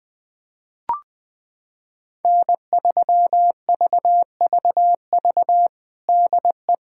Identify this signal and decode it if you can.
Morse code, heard as a keyed tone, about 20 wpm, 700 Hz. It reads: N3VVV DE
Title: USA Callsign DE Callsign - 5 Characters - 20wpm